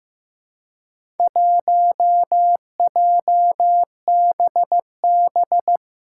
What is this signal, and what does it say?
Morse code, heard as a keyed tone, about 15 wpm, 700 Hz. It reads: 1JBB
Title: Four Challenging Letter-Number Combinations 15wpm